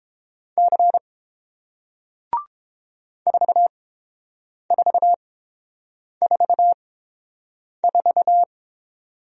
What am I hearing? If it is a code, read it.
Morse code, heard as a keyed tone, about 31 wpm, spaced as for 7 wpm, 700 Hz. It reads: C 4444